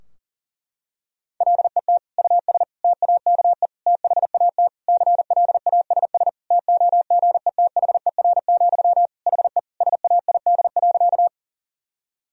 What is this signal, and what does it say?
Morse code, heard as a keyed tone, about 37 wpm, 700 Hz. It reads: LET US TAKE THAT CLASS TOGETHER, HE SAID.
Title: Sentences from Top 1000 Words - Review Part 2 - 40wpm